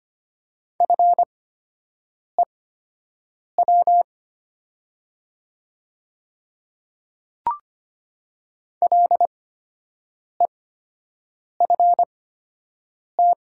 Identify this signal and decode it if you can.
Morse code, heard as a keyed tone, about 25 wpm, spaced as for 7 wpm, 700 Hz. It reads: FEW LEFT
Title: 8x Character Spacing - Top 300 Words - 25wpm